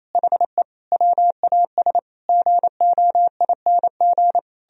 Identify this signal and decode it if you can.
Morse code, heard as a keyed tone, about 33 wpm, 700 Hz. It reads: HE WAS GOING
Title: Speed-Racing - Sets of 5 Words 28wpm